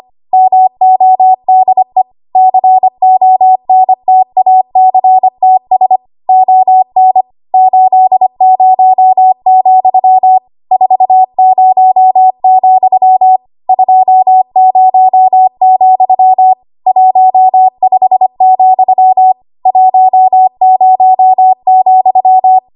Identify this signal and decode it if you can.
Morse code, heard as a keyed tone, about 25 wpm, 750 Hz. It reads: MODE CONTACTS ON 80, 40, 20, 15, 10,
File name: Archive / 25 WPM / 25WPM